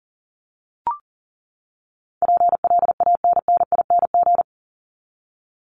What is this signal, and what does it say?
Morse code, heard as a keyed tone, about 40 wpm, 700 Hz. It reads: PLANNING
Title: ING Suffix - Top 100 Words with Base Words 40wpm